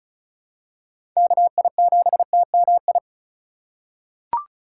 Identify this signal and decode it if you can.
Morse code, heard as a keyed tone, about 36 wpm, 700 Hz. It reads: KI7TMI
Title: USA Call Signs - 6 Characters - 4x Character Spacing - 35wpm